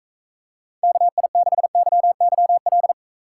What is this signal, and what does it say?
Morse code, heard as a keyed tone, about 43 wpm, 700 Hz. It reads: KI6YYL